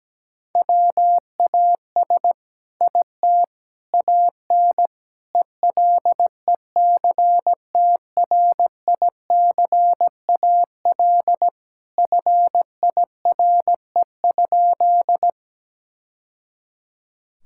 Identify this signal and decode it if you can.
Morse code, heard as a keyed tone, about 17 wpm, 700 Hz. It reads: WAS IT AN ELECTRICAL FIRE?
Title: Sentences from Top 1000 Words - Review Part 6 - 17wpm